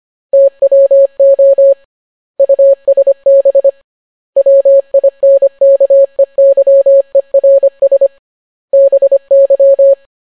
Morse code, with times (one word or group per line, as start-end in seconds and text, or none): 0.3-1.7 TWO
2.4-3.7 USB
4.4-8.1 WINKEYERS
8.7-9.9 BY